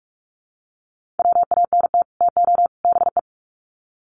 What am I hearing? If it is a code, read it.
Morse code, heard as a keyed tone, about 44 wpm, 700 Hz. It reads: WANT TO BE
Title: Sets of 3 Words 45wpm